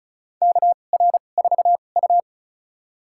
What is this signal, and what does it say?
Morse code, heard as a keyed tone, about 34 wpm, spaced as for 25 wpm, 700 Hz. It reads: KR4U
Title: USA Call Signs - 4 Characters - 2x Character Spacing - 35wpm